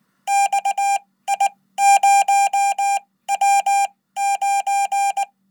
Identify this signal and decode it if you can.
Morse code, heard as a keyed tone, about 18 wpm, spaced as for 15 wpm, 800 Hz. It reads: XI0W9
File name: Code